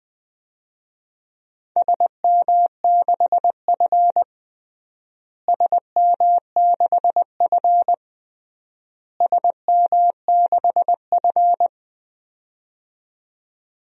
Morse code, heard as a keyed tone, about 20 wpm, 700 Hz. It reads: SM6F SM6F SM6F
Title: DXCC-100 Call Signs - 4 Characters - Repeated 3x - 20wpm